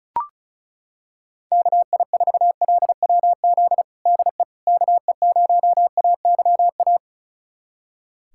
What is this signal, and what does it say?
Morse code, heard as a keyed tone, about 34 wpm, 700 Hz. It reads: KI4LWZ DE KE0AYA
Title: USA Callsign DE Callsign - 6 Characters - 35wpm